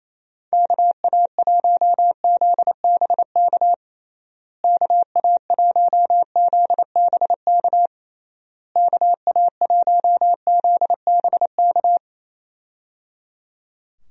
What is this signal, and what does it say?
Morse code, heard as a keyed tone, about 27 wpm, 700 Hz. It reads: KA1ZBK KA1ZBK KA1ZBK
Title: USA Call Signs - 6 Characters - Repeated 3x - 28wpm